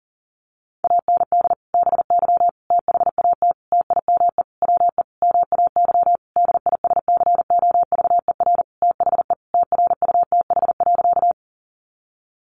Morse code, heard as a keyed tone, about 37 wpm, 700 Hz. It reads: AND BY THAT TIME WE MAY DISCOVER THE TRUTH.
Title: Sentences from Top 1100 Words 40wpm